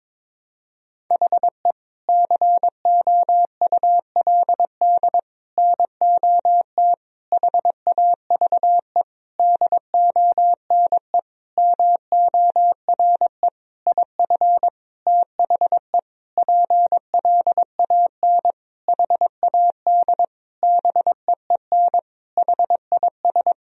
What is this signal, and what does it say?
Morse code, heard as a keyed tone, about 22 wpm, 700 Hz. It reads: HE COULD NOT HAVE DONE MORE IF THE PLAN HAD BEEN HIS
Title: Sentences from Top 1000 Words - Review Part 2 - 22wpm